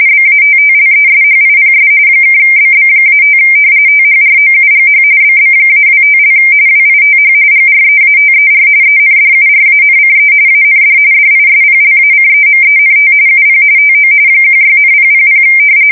RTTY, about 45.45 baud. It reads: BATTLESHIPS (BB-5 THROUGH BB-25) WITH GREATER ENDURANCE AND DISPLACEMENT UP TO 16,000
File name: fsk170 / battleshipnj